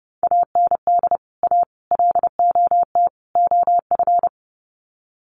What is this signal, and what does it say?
Morse code, heard as a keyed tone, about 30 wpm, 700 Hz. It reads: AND A LOT OF
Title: Sets of 4 Words 30wpm